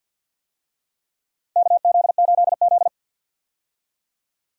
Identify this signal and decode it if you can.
Morse code, heard as a keyed tone, about 50 wpm, 700 Hz. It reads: KZ8Z
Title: USA Call Signs - 4 Characters - 50wpm